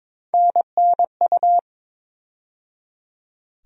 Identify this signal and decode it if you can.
Morse code, heard as a keyed tone, about 22 wpm, 700 Hz. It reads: NNU
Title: USA Call Sign Postfixes - 3 Letters 22wpm